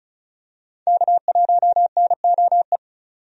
Morse code, heard as a keyed tone, about 36 wpm, 700 Hz. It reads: K1NOE
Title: USA Call Signs - 5 Characters - 35wpm